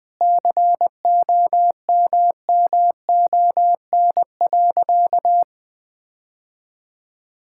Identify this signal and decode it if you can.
Morse code, heard as a keyed tone, about 21 wpm, 700 Hz. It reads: COMMON.
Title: Sentences with 6-Word Phrases - Spoken Beginning - 20wpm